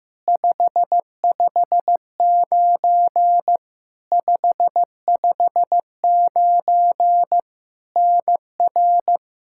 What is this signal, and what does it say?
Morse code, heard as a keyed tone, about 15 wpm, 700 Hz. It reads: 559 559 NR
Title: Practice QSO - Exchange 1 - Beginning - 15wpm